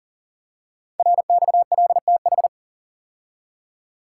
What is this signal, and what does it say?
Morse code, heard as a keyed tone, about 40 wpm, 700 Hz. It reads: RXLTH
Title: Code Groups - 5 Letters - 40wpm